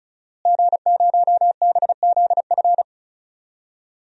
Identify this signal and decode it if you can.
Morse code, heard as a keyed tone, about 36 wpm, 700 Hz. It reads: G0BZF
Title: DXCC-100 Call Signs - 5 Characters - Repeated 2x - 35wpm